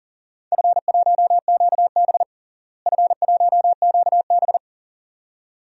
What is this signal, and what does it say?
Morse code, heard as a keyed tone, about 40 wpm, 700 Hz. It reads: F1QB F1QB
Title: DXCC-100 Call Signs - 4 Characters - Repeated 2x - 40wpm